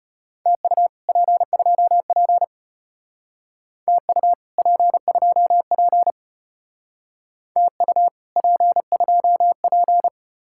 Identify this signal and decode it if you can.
Morse code, heard as a keyed tone, about 32 wpm, 700 Hz. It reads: TU P2P TU P2P TU P2P